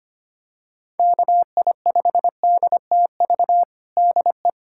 Morse code, heard as a keyed tone, about 25 wpm, 700 Hz. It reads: KI5DTV DE
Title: USA Callsign DE Callsign - 6 Characters - 25wpm